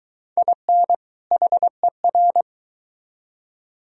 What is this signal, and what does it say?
Morse code, heard as a keyed tone, about 23 wpm, 700 Hz. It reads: IN HER